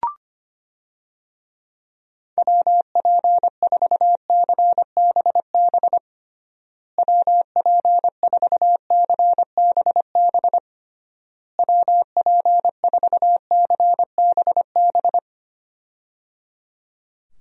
Morse code, heard as a keyed tone, about 25 wpm, 700 Hz. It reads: WP4CBB WP4CBB WP4CBB
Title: USA Call Signs - 6 Characters - Repeated 3x - 25wpm